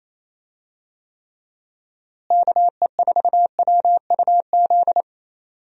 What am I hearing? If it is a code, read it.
Morse code, heard as a keyed tone, about 27 wpm, 700 Hz. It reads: KE4WUZ
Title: USA Call Signs - 6 Characters - Repeated 3x - 28wpm